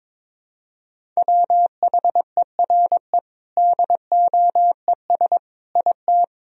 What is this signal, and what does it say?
Morse code, heard as a keyed tone, about 22 wpm, 700 Hz. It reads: WHERE DOES IT
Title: Sentences from Top 1000 Words - Review Part 2 - 22wpm